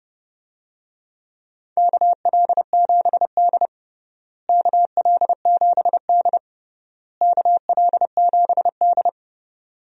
Morse code, heard as a keyed tone, about 32 wpm, 700 Hz. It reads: KL7D KL7D KL7D
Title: DXCC-100 Call Signs - 4 Characters - Repeated 3x - 30wpm